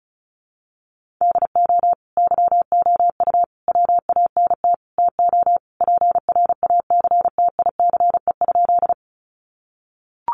Morse code, heard as a keyed tone, about 34 wpm, 700 Hz. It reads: DO YOU WANT TO PRACTICE?
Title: Sentences from Top 700 Words 35wpm